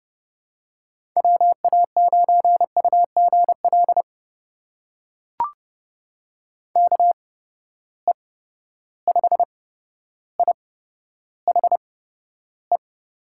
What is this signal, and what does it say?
Morse code, heard as a keyed tone, about 29 wpm, spaced as for 13 wpm, 700 Hz. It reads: WA9UGL KE5IHE
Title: USA Call Signs - 6 Characters - 8x Character Spacing - 30wpm